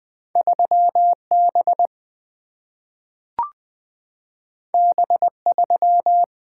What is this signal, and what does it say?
Morse code, heard as a keyed tone, about 20 wpm, 700 Hz. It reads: 3B B3
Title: Single-Letter Single-Number 20wpm